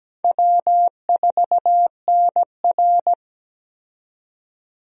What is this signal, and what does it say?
Morse code, heard as a keyed tone, about 17 wpm, 700 Hz. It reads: W4NR
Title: USA Call Signs - 4 Characters - 17wpm